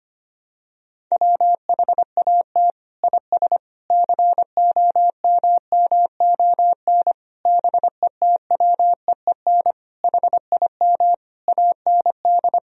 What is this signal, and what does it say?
Morse code, heard as a keyed tone, about 25 wpm, 700 Hz. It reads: WHAT IS COMMON BETWEEN HIM AND
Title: Sentences from Top 1000 Words - Review Part 3 - 25wpm